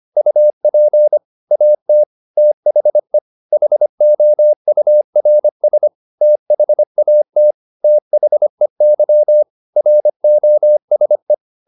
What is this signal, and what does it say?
Morse code, heard as a keyed tone, about 25 wpm, 600 Hz. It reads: UP AT THE HOURS THAT THEY ROSE
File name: the old man and the sea part 1 25wpm